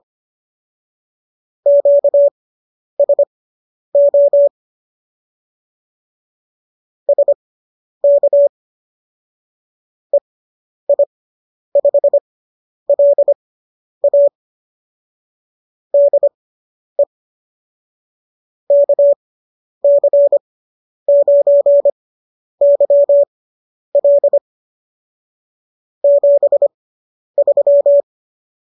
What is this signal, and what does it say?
Morse code, heard as a keyed tone, about 25 wpm, spaced as for 10 wpm, 590 Hz. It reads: QSO SK EI5LA DE KC9YL 73